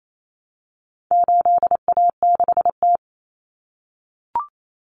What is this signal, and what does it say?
Morse code, heard as a keyed tone, about 28 wpm, 700 Hz. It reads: 8A6T
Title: Common Cut Numbers - 4-Digit 28wpm